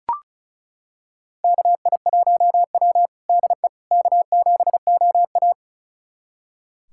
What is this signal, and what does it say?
Morse code, heard as a keyed tone, about 34 wpm, 690 Hz. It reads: KI1W DE K7OA
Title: USA Callsign DE Callsign - 4 Characters - 35wpm